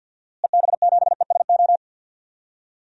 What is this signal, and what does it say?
Morse code, heard as a keyed tone, about 49 wpm, 700 Hz. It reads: EB7ESO
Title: DXCC-100 Call Signs - 6 Characters - Repeated 2x - 50wpm